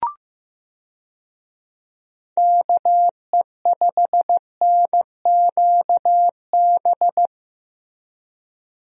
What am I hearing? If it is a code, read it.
Morse code, heard as a keyed tone, about 15 wpm, 700 Hz. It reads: KE5NQB
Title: USA Call Signs - 6 Characters - Repeated 2x - 15wpm